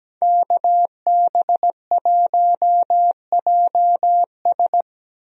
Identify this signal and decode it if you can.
Morse code, heard as a keyed tone, about 17 wpm, 700 Hz. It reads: KB1JS
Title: USA Call Signs - 5 Characters - 17wpm